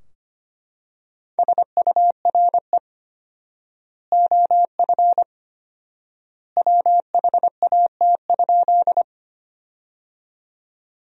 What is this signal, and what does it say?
Morse code, heard as a keyed tone, about 25 wpm, 700 Hz. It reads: SURE OF WHAT?
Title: 4x Word-Spacing - Sentences from Top 300 Words - 25wpm